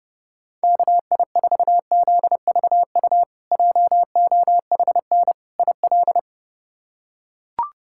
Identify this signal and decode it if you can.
Morse code, heard as a keyed tone, about 32 wpm, 700 Hz. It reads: KI4ZVU JOHN IL
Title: K1USN SST QSO Practice - 30wpm